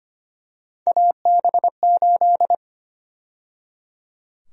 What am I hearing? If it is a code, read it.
Morse code, heard as a keyed tone, about 25 wpm, 700 Hz. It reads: AB8
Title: USA Call Sign Prefixes - 25wpm